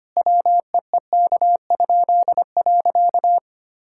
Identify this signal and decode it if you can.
Morse code, heard as a keyed tone, about 25 wpm, 700 Hz. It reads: WEEK?.
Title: Pangram Challenge v2 - 25wpm